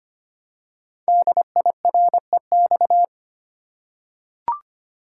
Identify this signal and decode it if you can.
Morse code, heard as a keyed tone, about 25 wpm, 700 Hz. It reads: DIREX
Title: Code Groups - 5 Letters - 25wpm